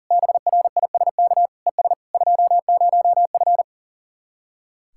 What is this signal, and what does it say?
Morse code, heard as a keyed tone, about 39 wpm, 700 Hz. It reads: BRISK ES 20F
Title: Speed-Racing Common QSO Phrases - 40wpm